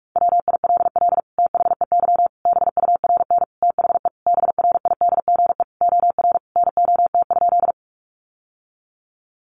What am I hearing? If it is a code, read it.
Morse code, heard as a keyed tone, about 44 wpm, 700 Hz. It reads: WILL THEY BURN THE BRIDGE OR NOT?
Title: Sentences from Top 1300 Words 45wpm